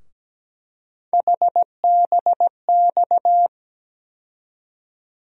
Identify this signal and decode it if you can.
Morse code, heard as a keyed tone, about 17 wpm, 700 Hz. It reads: HBX
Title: USA Call Sign Postfixes - 3 Letters 17wpm